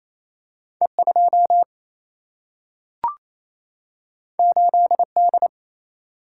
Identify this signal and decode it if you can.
Morse code, heard as a keyed tone, about 28 wpm, 700 Hz. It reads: E2 8D
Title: Single-Letter Single-Number 28wpm